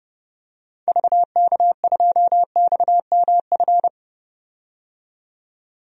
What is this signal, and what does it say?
Morse code, heard as a keyed tone, about 30 wpm, 700 Hz. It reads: VK2XMF
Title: DXCC-100 Call Signs - 6 Characters - 30wpm